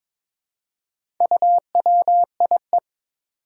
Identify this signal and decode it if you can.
Morse code, heard as a keyed tone, about 21 wpm, 700 Hz. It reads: UWIE
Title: Four Challenging Letter-Number Combinations 22wpm